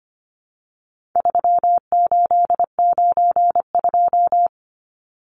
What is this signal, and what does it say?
Morse code, heard as a keyed tone, about 25 wpm, 700 Hz. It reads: 3892
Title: Numbers - 4-Digit 25wpm